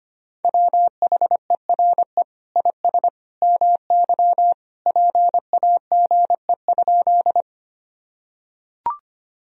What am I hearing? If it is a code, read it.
Morse code, heard as a keyed tone, about 25 wpm, 700 Hz. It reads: WHERE IS MY PAGE?
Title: Sentences from Top 1000 Words - Review Part 1 - 25wpm